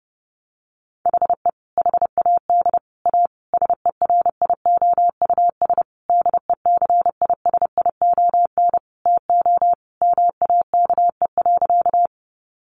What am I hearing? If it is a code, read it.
Morse code, heard as a keyed tone, about 30 wpm, 700 Hz. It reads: HE HAD A SERIOUS DECISION TO MAKE.
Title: Sentences from Top 1100 Words 30wpm